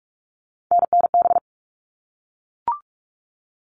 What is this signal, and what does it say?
Morse code, heard as a keyed tone, about 44 wpm, 700 Hz. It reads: NNB
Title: All Cut Numbers - 3-Digit 45wpm